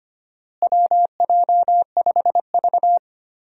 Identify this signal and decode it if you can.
Morse code, heard as a keyed tone, about 25 wpm, 700 Hz. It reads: WJ5V
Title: USA Call Signs - 4 Characters - Repeated 3x - 25wpm